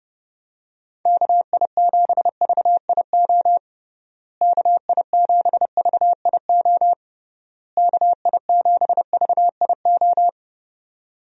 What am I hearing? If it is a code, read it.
Morse code, heard as a keyed tone, about 30 wpm, 700 Hz. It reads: KI7VIO KI7VIO KI7VIO
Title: USA Call Signs - 6 Characters - Repeated 3x - 30wpm